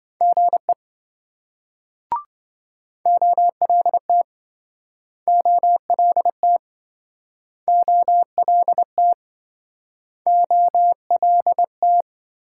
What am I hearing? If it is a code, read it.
Morse code, heard as a keyed tone, about 27 wpm, 700 Hz. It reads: GE OLT OLT OLT OLT